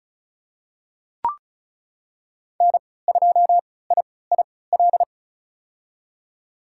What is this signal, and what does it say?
Morse code, heard as a keyed tone, about 34 wpm, spaced as for 20 wpm, 700 Hz. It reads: N2IIL
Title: USA Call Signs - 3x Character Spacing - 35wpm